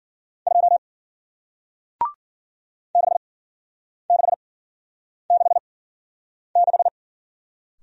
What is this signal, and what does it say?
Morse code, heard as a keyed tone, about 56 wpm, spaced as for 9 wpm, 700 Hz. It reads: 2 6666